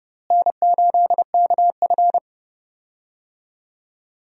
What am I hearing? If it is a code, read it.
Morse code, heard as a keyed tone, about 30 wpm, 700 Hz. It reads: N8KF
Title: USA Call Signs - 4 Characters - Repeated 2x - 30wpm